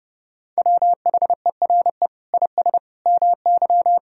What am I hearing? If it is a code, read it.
Morse code, heard as a keyed tone, about 30 wpm, 700 Hz. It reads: WHERE IS MY
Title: Sentences from Top 1000 Words - Review Part 1 - 30wpm